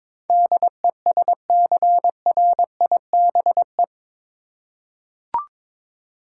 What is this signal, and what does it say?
Morse code, heard as a keyed tone, about 22 wpm, 700 Hz. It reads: DESCRIBE
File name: practice / 22wpm / Eight Letter Words - 22wpm